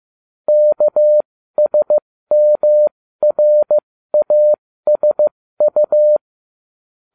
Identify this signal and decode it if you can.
Morse code, heard as a keyed tone, about 15 wpm, spaced as for 13 wpm, 600 Hz. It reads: KSMRASU